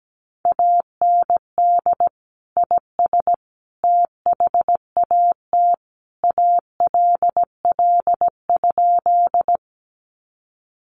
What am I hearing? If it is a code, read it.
Morse code, heard as a keyed tone, about 17 wpm, 700 Hz. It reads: AND IS THAT ALL?
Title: Sentences from Top 100 Words 17wpm